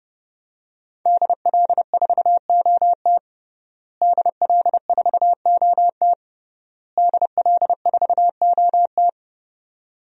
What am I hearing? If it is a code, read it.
Morse code, heard as a keyed tone, about 30 wpm, 700 Hz. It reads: DL4OT DL4OT DL4OT
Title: USA Call Signs - 5 Characters - Repeated 3x - 30wpm